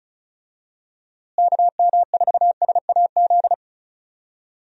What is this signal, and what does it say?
Morse code, heard as a keyed tone, about 36 wpm, 700 Hz. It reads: KM4SAZ
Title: USA Call Signs - 6 Characters - Repeated 2x - 35wpm